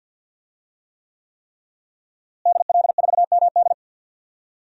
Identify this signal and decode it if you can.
Morse code, heard as a keyed tone, about 47 wpm, 700 Hz. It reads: DL4MD
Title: DXCC-100 Call Signs - 5 Characters - Repeated 3x - 50wpm